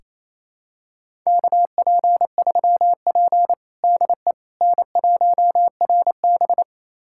Morse code, heard as a keyed tone, about 28 wpm, 700 Hz. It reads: KP3P DE N1RB
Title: USA Callsign DE Callsign - 4 Characters - 28wpm